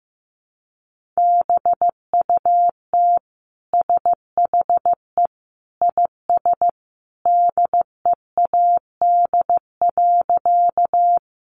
Morse code, heard as a keyed tone, about 15 wpm, 700 Hz. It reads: BUT SHE IS DEAD.
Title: Sentences from Top 1500 Words 15wpm